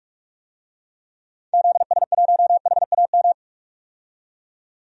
Morse code, heard as a keyed tone, about 46 wpm, 700 Hz. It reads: ZS1HAM
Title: DXCC-100 Call Signs - 6 Characters - 45wpm